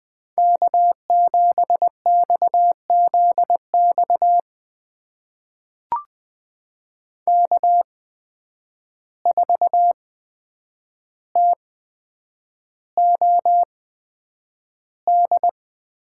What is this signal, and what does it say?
Morse code, heard as a keyed tone, about 20 wpm, spaced as for 9 wpm, 700 Hz. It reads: K7XZX K4TOD
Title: USA Call Signs - 5 Characters - 8x Character Spacing - 20wpm